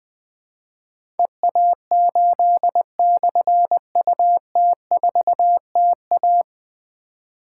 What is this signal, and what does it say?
Morse code, heard as a keyed tone, about 20 wpm, 700 Hz. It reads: EA8/UT4TA
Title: Difficult Call Signs 20wpm